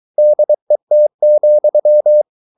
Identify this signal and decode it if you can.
Morse code, heard as a keyed tone, about 23 wpm, 600 Hz. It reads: DET,
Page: Ferie 23 wpm